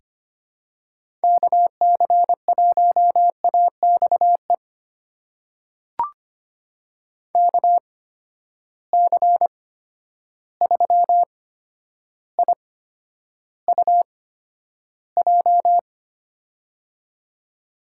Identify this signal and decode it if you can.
Morse code, heard as a keyed tone, about 25 wpm, spaced as for 11 wpm, 700 Hz. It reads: KC1AXE KC3IUJ
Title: USA Call Signs - 6 Characters - 8x Character Spacing - 25wpm